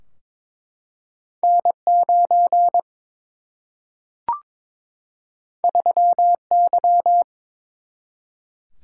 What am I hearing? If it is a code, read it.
Morse code, heard as a keyed tone, about 21 wpm, 700 Hz. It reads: N 9 3 Y
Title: Single-Letter Single-Number 22wpm